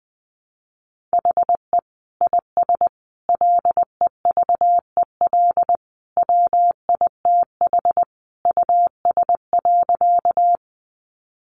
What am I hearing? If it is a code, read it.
Morse code, heard as a keyed tone, about 20 wpm, 700 Hz. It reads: HE IS LEVEL WITH US.
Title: Sentences from Top 700 Words 20wpm